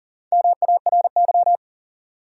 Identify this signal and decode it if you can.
Morse code, heard as a keyed tone, about 43 wpm, 700 Hz. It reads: MARY
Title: USA Common Names - Max 4 Letters - 40wpm